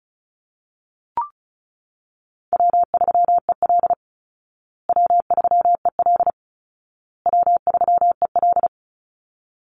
Morse code, heard as a keyed tone, about 34 wpm, 700 Hz. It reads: W3EL W3EL W3EL
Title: Call Signs Repeated 2x - 35wpm